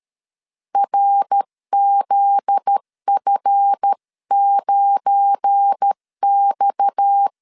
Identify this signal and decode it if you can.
Morse code, heard as a keyed tone, about 13 wpm, 800 Hz. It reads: RZF9X
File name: LEZIONI / 7 GF9Z / 7 GF9Z 12wpm 2147